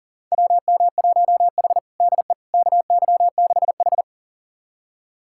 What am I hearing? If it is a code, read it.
Morse code, heard as a keyed tone, about 37 wpm, 700 Hz. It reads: WM1H DE KY6H
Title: USA Callsign DE Callsign - 4 Characters - 40wpm